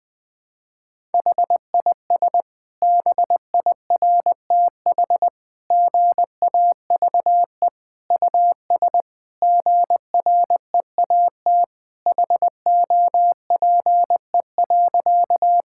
Morse code, heard as a keyed tone, about 20 wpm, 700 Hz. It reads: HIS BIRTH GAVE US GREAT HOPE.
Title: Sentences from Top 1000 Words - Review Part 5 - 20wpm